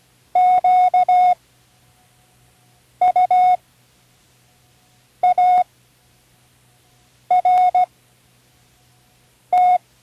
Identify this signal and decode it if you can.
Morse code, heard as a keyed tone, about 16 wpm, spaced as for 5 wpm, 730 Hz. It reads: QUART
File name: Learning Morse Code (from Tune in the World)